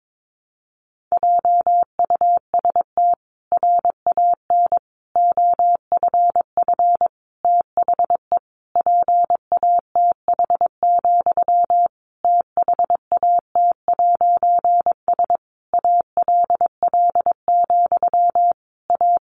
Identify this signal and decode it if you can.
Morse code, heard as a keyed tone, about 22 wpm, 700 Hz. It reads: JUST RAN OFF THE PATH, THAT'S ALL, A
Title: A Sound Of Thunder 22wpm